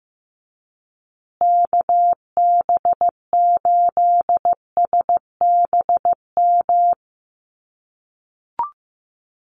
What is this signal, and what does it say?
Morse code, heard as a keyed tone, about 15 wpm, 700 Hz. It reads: KB8SBM
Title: USA Call Signs Repeated 2x 15wpm